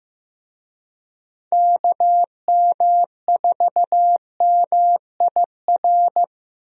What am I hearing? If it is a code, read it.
Morse code, heard as a keyed tone, about 15 wpm, 700 Hz. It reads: KM4MIR
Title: USA Call Signs - 6 Characters - Repeated 2x - 15wpm